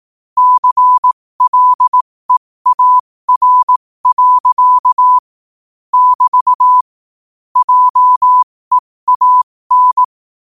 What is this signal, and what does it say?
Morse code, heard as a keyed tone, about 18 wpm, spaced as for 15 wpm, 1000 Hz. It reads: CLEAR. <BT> JEAN
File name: qotdfiles / 15wpm